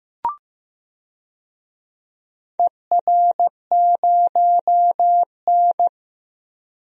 Morse code, heard as a keyed tone, about 15 wpm, 700 Hz. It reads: ER0N
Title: DXCC-100 Call Signs - 4 Characters - 15wpm